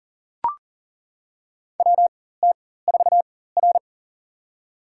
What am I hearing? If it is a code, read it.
Morse code, heard as a keyed tone, about 40 wpm, spaced as for 19 wpm, 700 Hz. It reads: WT4R
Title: USA Call Signs - 4 Characters - 4x Character Spacing - 40wpm